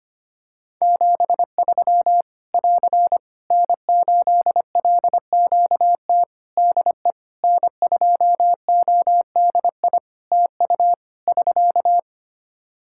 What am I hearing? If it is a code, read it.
Morse code, heard as a keyed tone, about 25 wpm, 700 Hz. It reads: 73 <AR> N8LQT DE N2ODI TU <SK>
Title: K1USN SST QSO Practice 25wpm